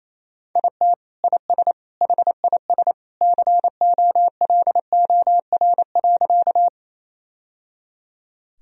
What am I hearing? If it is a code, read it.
Morse code, heard as a keyed tone, about 28 wpm, 700 Hz. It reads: IT IS HIS COLOR.
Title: Sentences from Top 1000 Words - Review Part 2 - 28wpm